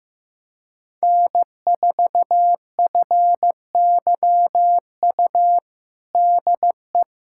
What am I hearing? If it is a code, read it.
Morse code, heard as a keyed tone, about 16 wpm, 700 Hz. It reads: N4FYU DE
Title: USA Callsign DE Callsign - 5 Characters - 15wpm